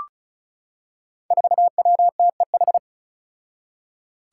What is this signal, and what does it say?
Morse code, heard as a keyed tone, about 34 wpm, 700 Hz. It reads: 4WTEH
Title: Five Challenging Letter-Number Combinations 35wpm